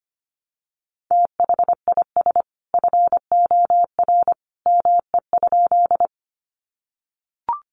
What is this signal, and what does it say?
Morse code, heard as a keyed tone, about 25 wpm, 700 Hz. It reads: THIS FOR ME?
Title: Sentences from Top 200 Words 25wpm